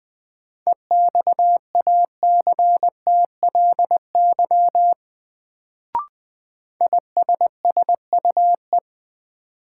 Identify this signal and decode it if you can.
Morse code, heard as a keyed tone, about 21 wpm, 700 Hz. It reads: EXACTLY ISSUE
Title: Sentences from Top 900 Words - New Word Drill - 20wpm